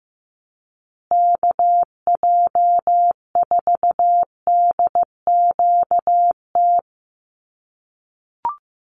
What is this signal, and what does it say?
Morse code, heard as a keyed tone, about 15 wpm, 700 Hz. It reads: KJ4DQT
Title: USA Call Signs Repeated 3x 15wpm